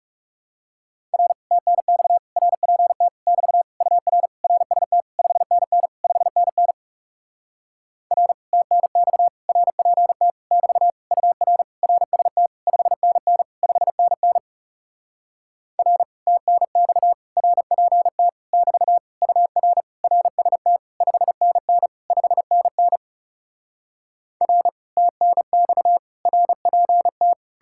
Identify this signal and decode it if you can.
Morse code, heard as a keyed tone, about 40 wpm, 700 Hz. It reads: R TNX RPT <BT> UR RST 5NN 5NN R TNX RPT <BT> UR RST 5NN 5NN R TNX RPT <BT> UR RST 5NN 5NN R TNX RPT